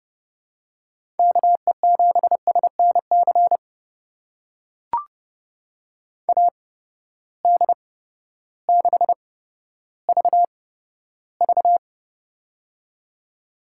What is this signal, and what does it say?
Morse code, heard as a keyed tone, about 31 wpm, spaced as for 14 wpm, 700 Hz. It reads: KE7SNC AD6VV
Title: USA Call Signs - 8x Character Spacing - 30wpm